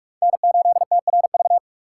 Morse code, heard as a keyed tone, about 44 wpm, 700 Hz. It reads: N8TRV
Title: USA Call Signs - 6x Character Spacing - 45wpm